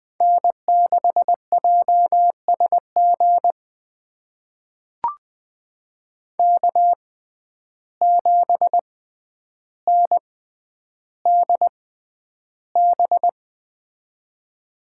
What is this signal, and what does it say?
Morse code, heard as a keyed tone, about 20 wpm, spaced as for 10 wpm, 700 Hz. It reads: N6JSG K7NDB
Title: USA Call Signs - 5 Characters - 6x Character Spacing - 20wpm